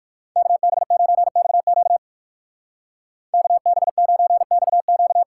Morse code, heard as a keyed tone, about 51 wpm, 700 Hz. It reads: KB9XQ KB9XQ
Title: Speed Racing - USA 2x2 Call Signs 45wpm